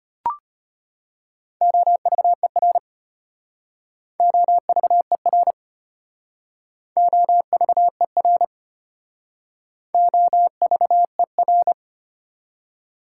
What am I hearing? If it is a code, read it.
Morse code, heard as a keyed tone, about 32 wpm, 700 Hz. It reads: OVER OVER OVER OVER